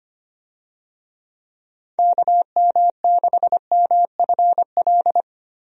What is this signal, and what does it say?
Morse code, heard as a keyed tone, about 25 wpm, 700 Hz. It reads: KM6MFL
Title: USA Call Signs - 6 Characters - Repeated 2x - 25wpm